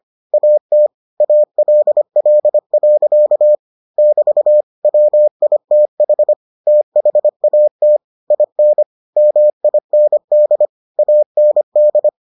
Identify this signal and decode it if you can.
Morse code, heard as a keyed tone, about 25 wpm, 600 Hz. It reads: AT ALL. <BT> WITH THAT IN MIND AND